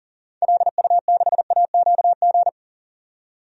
Morse code, heard as a keyed tone, about 40 wpm, 700 Hz. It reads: LU6AQG
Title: DXCC-100 Call Signs - 6 Characters - 40wpm